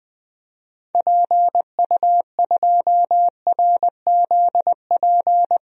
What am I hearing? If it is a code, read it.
Morse code, heard as a keyed tone, about 20 wpm, 700 Hz. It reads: PU2RZP
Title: DXCC-100 Call Signs - 6 Characters - 20wpm